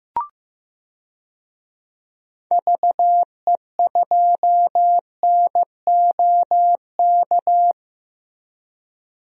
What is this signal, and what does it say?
Morse code, heard as a keyed tone, about 15 wpm, 700 Hz. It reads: VE2NOK
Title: DXCC-100 Call Signs - 6 Characters - 15wpm